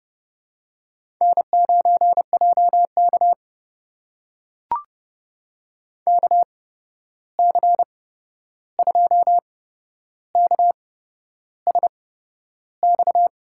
Morse code, heard as a keyed tone, about 30 wpm, spaced as for 11 wpm, 700 Hz. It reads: N9JK KC2KSX
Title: USA Call Signs - 8x Character Spacing - 30wpm